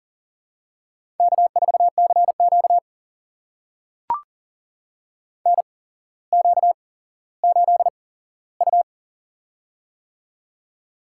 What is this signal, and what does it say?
Morse code, heard as a keyed tone, about 43 wpm, spaced as for 17 wpm, 700 Hz. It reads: K4CQ NQ8U
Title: USA Call Signs - 4 Characters - 8x Character Spacing - 40wpm